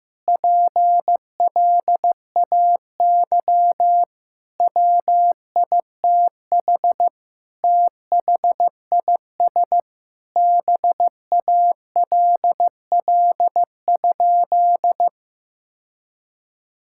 Morse code, heard as a keyed tone, about 15 wpm, 700 Hz. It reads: PLAY WITH THIS BALL?
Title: Sentences from Top 1000 Words - Review Part 6 - 15wpm